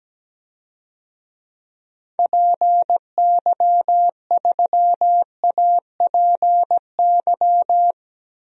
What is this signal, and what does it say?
Morse code, heard as a keyed tone, about 17 wpm, 700 Hz. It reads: PY3APY
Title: DXCC-100 Call Signs - 6 Characters - 17wpm